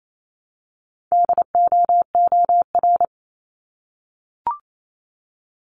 Morse code, heard as a keyed tone, about 28 wpm, 700 Hz. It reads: DOOR
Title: Multi-letter Phonemes Repeated 3x 28wpm